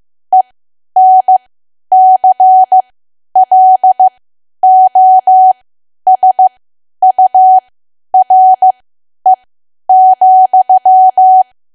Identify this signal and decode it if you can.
Morse code, heard as a keyed tone, about 15 wpm, spaced as for 10 wpm, 750 Hz. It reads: ENCLOSURE,
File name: Archive / 10 WPM / 10WPM